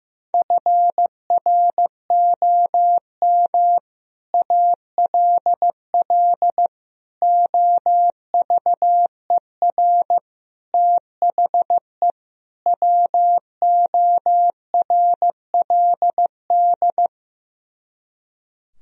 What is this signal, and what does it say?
Morse code, heard as a keyed tone, about 15 wpm, 700 Hz. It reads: FROM ALL OVER THE WORLD
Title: Speed-Racing - Sets of 5 Words 15wpm